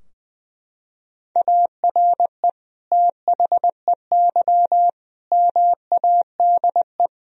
Morse code, heard as a keyed tone, about 20 wpm, 700 Hz. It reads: ARE THEY MADE
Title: Sentences from Top 1000 Words - Review Part 2 - 20wpm